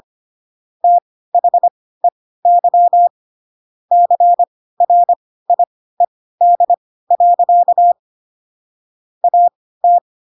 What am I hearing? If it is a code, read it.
Morse code, heard as a keyed tone, about 25 wpm, spaced as for 16 wpm, 700 Hz. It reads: THEY CRIED. AT
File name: Pinocchio / Pinocchio ch3 10 25 ik1hge